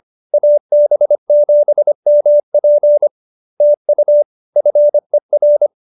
Morse code, heard as a keyed tone, about 25 wpm, 590 Hz. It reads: AB7MP TU FER
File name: QSO306 25